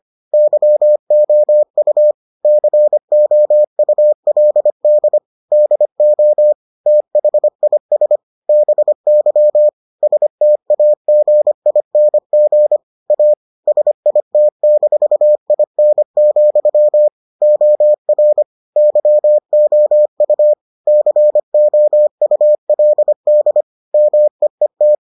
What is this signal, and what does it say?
Morse code, heard as a keyed tone, about 25 wpm, 600 Hz. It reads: YOU COULD DO THIS BY STAGING A SIT-IN, OR YOU COULD MEET